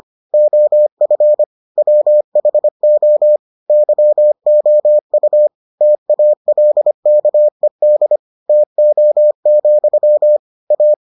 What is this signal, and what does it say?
Morse code, heard as a keyed tone, about 25 wpm, 600 Hz. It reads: OF WHO YOU TALKED TO, A